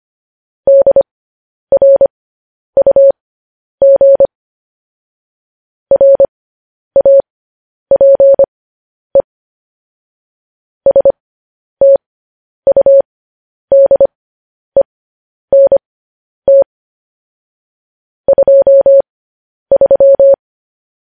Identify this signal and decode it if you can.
Morse code, heard as a keyed tone, about 25 wpm, spaced as for 10 wpm, 550 Hz. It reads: DRUG RAPE STUDENT 23
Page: news headlines in morse code at 10 wpm